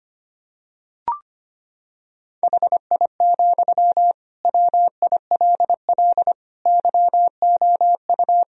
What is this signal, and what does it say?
Morse code, heard as a keyed tone, about 25 wpm, 700 Hz. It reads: HI, WILL YOU
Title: Sentences from Top 1000 Words - Review Part 3 - 25wpm